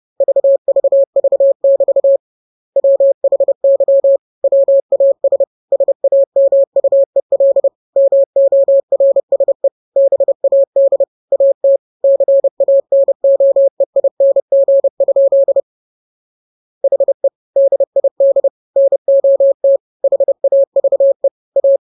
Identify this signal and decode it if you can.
Morse code, heard as a keyed tone, about 30 wpm, 550 Hz. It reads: VVV<BT> WHY WAS SAMUEL MORSE BAD AT CANOEING? HE DID NOT HAVE A
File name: jokes / joke 1 30 0000